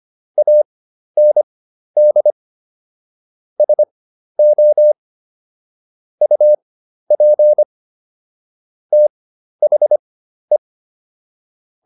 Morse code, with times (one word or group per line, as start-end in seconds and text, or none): 0.4-2.3 AND
3.6-4.9 SO
6.2-7.6 UP
8.9-10.6 THE